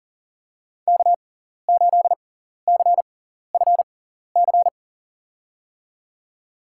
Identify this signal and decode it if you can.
Morse code, heard as a keyed tone, about 40 wpm, spaced as for 14 wpm, 700 Hz. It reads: K8CFC
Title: USA Call Signs - 5 Characters - 6x Character Spacing - 40wpm